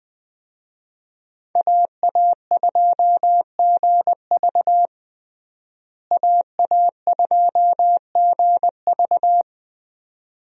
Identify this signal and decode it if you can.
Morse code, heard as a keyed tone, about 20 wpm, 700 Hz. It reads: AA2GV AA2GV
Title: USA Call Signs - 5 Characters - Repeated 2x - 20wpm